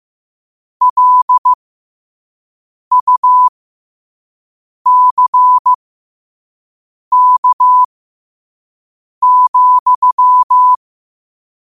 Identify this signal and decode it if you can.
Morse code, heard as a keyed tone, about 15 wpm, spaced as for 5 wpm, 1000 Hz. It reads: LUCK,
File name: qotdfiles / 5wpm